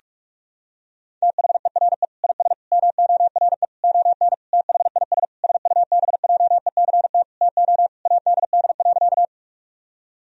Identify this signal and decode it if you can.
Morse code, heard as a keyed tone, about 46 wpm, 700 Hz. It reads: THERE IS MORE ON THIS SUBJECT TO ADD.
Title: Sentences from Top 1000 Words - Review Part 1 - 45wpm